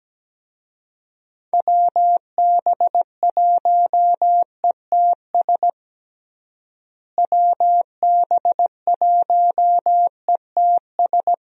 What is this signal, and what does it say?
Morse code, heard as a keyed tone, about 17 wpm, 700 Hz. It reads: WB1ETS WB1ETS
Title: USA Call Signs - 6 Characters - Repeated 2x - 17wpm